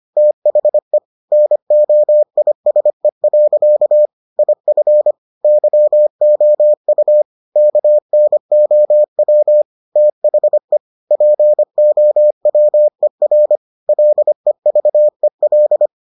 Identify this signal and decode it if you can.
Morse code, heard as a keyed tone, about 25 wpm, 600 Hz. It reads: THE NOISE. IF YOU KNOW THE POWER LEVEL